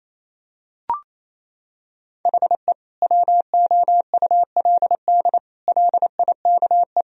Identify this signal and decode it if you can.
Morse code, heard as a keyed tone, about 28 wpm, 700 Hz. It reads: HE WOULD LIKE
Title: Sentences from Top 1000 Words - Review Part 3 - 28wpm